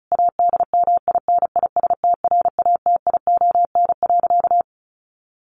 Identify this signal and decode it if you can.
Morse code, heard as a keyed tone, about 34 wpm, 700 Hz. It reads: ADMINISTRATION.
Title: Sentences from Top 1300 Words 35wpm